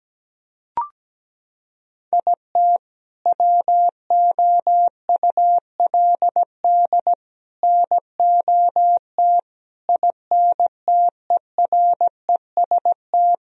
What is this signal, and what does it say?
Morse code, heard as a keyed tone, about 17 wpm, 700 Hz. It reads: IT WOULD NOT INTEREST
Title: Sentences from Top 1000 Words - Review Part 2 - 17wpm